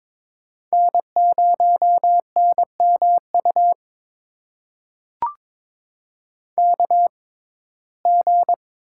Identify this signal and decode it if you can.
Morse code, heard as a keyed tone, about 23 wpm, spaced as for 16 wpm, 700 Hz. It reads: N0NMU KG
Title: USA Call Signs - 6x Character Spacing - 22wpm